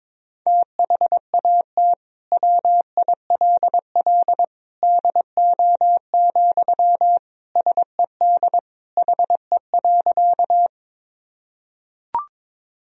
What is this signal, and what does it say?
Morse code, heard as a keyed tone, about 22 wpm, 700 Hz. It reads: THAT WILL DO, SED HE.
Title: Sentences from Top 100 Words and Abbreviations 22wpm